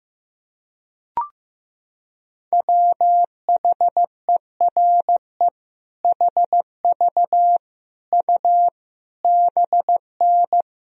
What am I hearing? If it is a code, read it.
Morse code, heard as a keyed tone, about 15 wpm, 700 Hz. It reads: WHERE HV U BN
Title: Sentences from Top 300 Words and Abbreviations 15wpm